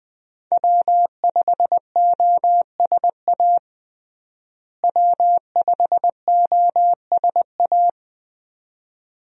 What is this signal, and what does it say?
Morse code, heard as a keyed tone, about 21 wpm, 700 Hz. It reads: W5OSA W5OSA
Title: USA Call Signs - 5 Characters - Repeated 2x - 20wpm